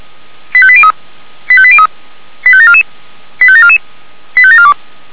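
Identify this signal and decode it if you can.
Five-tone selective call (ZVEI), heard as quick runs of five tones, tones of about 70 ms. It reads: 85803 85803 85744 85744 85742
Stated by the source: ZVEI five-tone call 85730 received on SDR by Feuerwehr Orlamünde.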